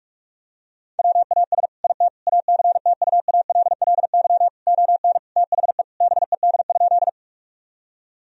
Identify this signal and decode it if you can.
Morse code, heard as a keyed tone, about 46 wpm, 700 Hz. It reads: WAS IT ACTUALLY ON THE BED?
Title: Sentences from Top 1000 Words - Review Part 4 - 45wpm